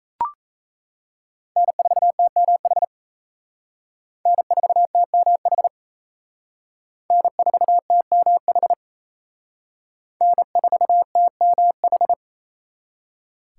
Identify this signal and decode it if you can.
Morse code, heard as a keyed tone, about 35 wpm, 700 Hz. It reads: N4TMH N4TMH N4TMH N4TMH